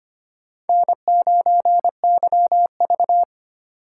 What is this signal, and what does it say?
Morse code, heard as a keyed tone, about 25 wpm, 700 Hz. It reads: N9YV
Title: USA Call Signs - 4 Characters - 2x Character Spacing - 25wpm